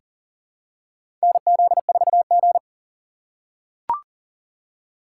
Morse code, heard as a keyed tone, about 40 wpm, 700 Hz. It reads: NZ4G
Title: USA Call Signs - 2x Character Spacing - 40wpm